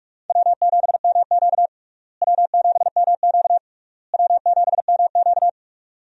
Morse code, heard as a keyed tone, about 42 wpm, 700 Hz. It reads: W7MQ W7MQ W7MQ
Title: USA Call Signs - 4 Characters - Repeated 3x - 45wpm